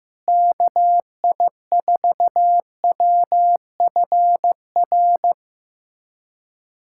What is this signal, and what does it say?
Morse code, heard as a keyed tone, about 15 wpm, 700 Hz. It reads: KI4WFR
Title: USA Callsign DE Callsign 15wpm